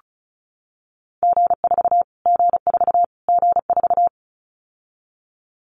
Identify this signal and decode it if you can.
Morse code, heard as a keyed tone, about 34 wpm, 700 Hz. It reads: G4 G4 G4
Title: Callsign Prefixes for Top 25 DXCC - Repeated 3x - 35wpm